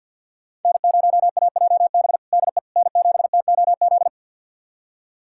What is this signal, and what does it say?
Morse code, heard as a keyed tone, about 52 wpm, 690 Hz. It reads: N0AJB DE N7TOZ
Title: USA Callsign DE Callsign - 5 Characters - 50wpm